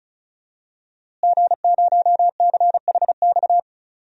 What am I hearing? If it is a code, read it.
Morse code, heard as a keyed tone, about 34 wpm, 700 Hz. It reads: G0CHX
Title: DXCC-100 Call Signs - 5 Characters - Repeated 2x - 35wpm